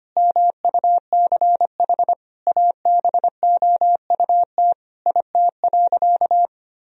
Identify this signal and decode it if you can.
Morse code, heard as a keyed tone, about 25 wpm, 700 Hz. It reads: MUCH ABOUT IT.
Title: Sentences from Top 1000 Words - Review Part 2 - 25wpm